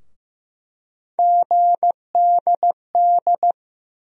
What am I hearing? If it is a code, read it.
Morse code, heard as a keyed tone, about 15 wpm, 700 Hz. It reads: GDD
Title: USA Call Sign Postfixes - 3 Letters 15wpm